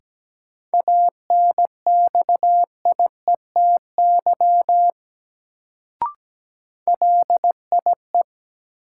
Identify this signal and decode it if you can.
Morse code, heard as a keyed tone, about 17 wpm, 700 Hz. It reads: ANXIETY LIE
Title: Sentences from Top 1600 Words - New Word Drill - 17wpm